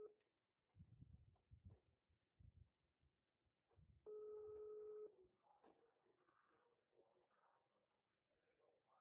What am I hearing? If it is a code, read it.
Ringback tone (Europe, 425 Hz).
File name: out- -2021- - - .2409